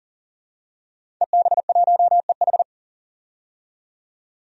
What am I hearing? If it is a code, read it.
Morse code, heard as a keyed tone, about 40 wpm, 700 Hz. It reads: EB1EH
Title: DXCC-100 Call Signs - 5 Characters - 40wpm